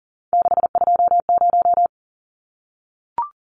Morse code, heard as a keyed tone, about 40 wpm, 700 Hz. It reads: 620
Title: Numbers - 3-Digit 40wpm